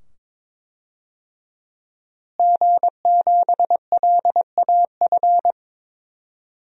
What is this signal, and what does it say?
Morse code, heard as a keyed tone, about 22 wpm, 700 Hz. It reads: G7LAF
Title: DXCC-100 Call Signs - 5 Characters - Repeated 2x - 22wpm